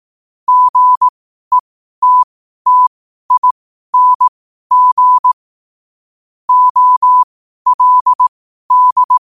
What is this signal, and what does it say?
Morse code, heard as a keyed tone, about 18 wpm, spaced as for 12 wpm, 1000 Hz. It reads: GETTING OLD
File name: qotdfiles / 12wpm